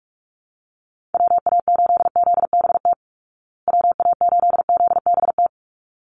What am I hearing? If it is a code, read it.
Morse code, heard as a keyed tone, about 44 wpm, 700 Hz. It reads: WA8ZBT WA8ZBT
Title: Call Signs Repeated 3x - 45wpm